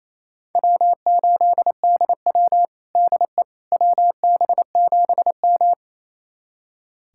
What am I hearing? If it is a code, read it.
Morse code, heard as a keyed tone, about 28 wpm, 700 Hz. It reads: W8DW DE WB7M
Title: USA Callsign DE Callsign - 4 Characters - 28wpm